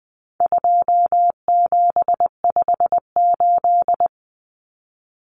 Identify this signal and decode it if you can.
Morse code, heard as a keyed tone, about 20 wpm, 700 Hz. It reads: 2758
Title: Numbers - 4-Digit 20wpm